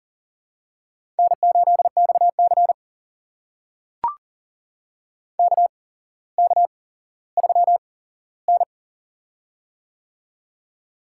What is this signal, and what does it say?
Morse code, heard as a keyed tone, about 40 wpm, spaced as for 17 wpm, 700 Hz. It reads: N8XC KK3N
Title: USA Call Signs - 4 Characters - 8x Character Spacing - 40wpm